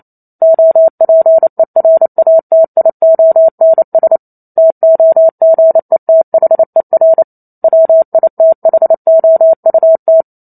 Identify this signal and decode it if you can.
Morse code, heard as a keyed tone, about 28 wpm, 650 Hz. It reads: OPERATIONS TOGETHER WITHOUT
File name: Chapter 2 - 28 WPM